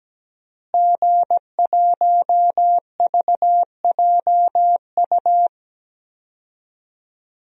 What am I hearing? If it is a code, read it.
Morse code, heard as a keyed tone, about 17 wpm, 700 Hz. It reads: G1VJU
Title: DXCC-100 Call Signs - 5 Characters - Repeated 3x - 17wpm